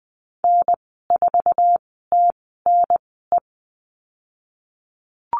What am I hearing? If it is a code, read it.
Morse code, heard as a keyed tone, about 20 wpm, spaced as for 15 wpm, 700 Hz. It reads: N4TNE
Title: 048 Lesson 9g - Number 4 - Call Signs 20wpm